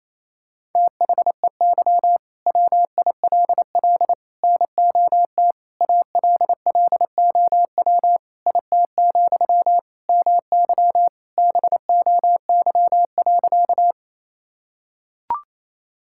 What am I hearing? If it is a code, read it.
Morse code, heard as a keyed tone, about 28 wpm, 700 Hz. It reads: THEY WILL NOT ALLOW IT, MY BOY.
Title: Sentences from Top 1000 Words - Review Part 5 - 28wpm